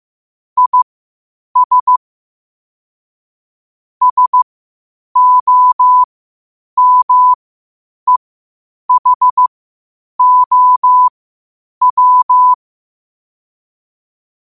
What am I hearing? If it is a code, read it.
Morse code, heard as a keyed tone, about 15 wpm, spaced as for 8 wpm, 1000 Hz. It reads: IS SOMEHOW
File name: qotdfiles / 8wpm